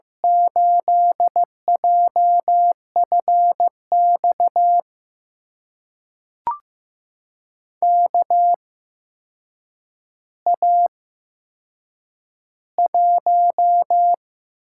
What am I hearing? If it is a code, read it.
Morse code, heard as a keyed tone, about 15 wpm, spaced as for 7 wpm, 700 Hz. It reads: 8JFX KA1
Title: USA Call Signs - 6 Characters - 8x Character Spacing - 15wpm